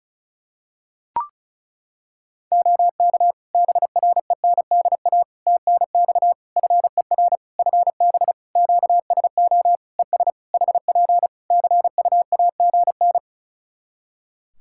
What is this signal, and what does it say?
Morse code, heard as a keyed tone, about 34 wpm, 700 Hz. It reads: OK BRENDA TNX FER FB QSO ES HP CUAGN
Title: Practice QSO - Exchange 4 - Ending - 35wpm